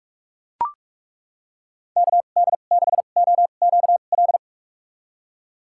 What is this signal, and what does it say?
Morse code, heard as a keyed tone, about 41 wpm, spaced as for 31 wpm, 700 Hz. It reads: KD6OQL
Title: USA Call Signs - 2x Character Spacing - 45wpm